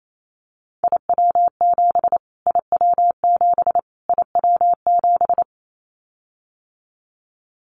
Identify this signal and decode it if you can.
Morse code, heard as a keyed tone, about 28 wpm, 700 Hz. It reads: IW7 IW7 IW7
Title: Callsign Prefixes for Top 25 DXCC - Repeated 3x - 28wpm